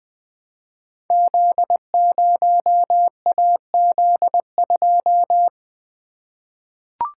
Morse code, heard as a keyed tone, about 20 wpm, 700 Hz. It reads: Z0AZ2
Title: Code Groups - 5 Letter:Numbers - 20wpm